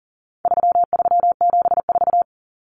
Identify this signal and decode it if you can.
Morse code, heard as a keyed tone, about 40 wpm, 700 Hz. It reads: 3374
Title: Numbers - 4-Digit 40wpm